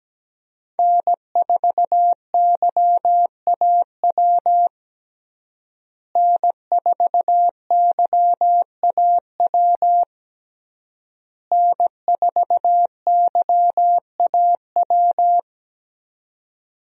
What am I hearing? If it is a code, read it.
Morse code, heard as a keyed tone, about 17 wpm, 700 Hz. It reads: N4YAW N4YAW N4YAW
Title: USA Call Signs - 5 Characters - Repeated 3x - 17wpm